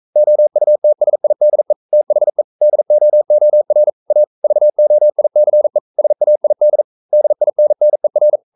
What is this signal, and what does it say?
Morse code, heard as a keyed tone, about 42 wpm, 600 Hz. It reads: OUTSIDE THE DOOR A VOICE SAID DINNER
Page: Romanen Never af Ken Follet på engelsk.